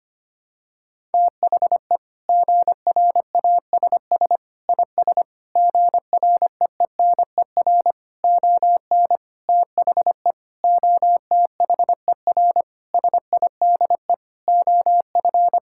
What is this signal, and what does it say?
Morse code, heard as a keyed tone, about 25 wpm, 700 Hz. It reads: THE GRASS IS GREENER ON THE OTHER SIDE OF
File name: practice / 25wpm / Sentences with 4-Word Phrases - 25wpm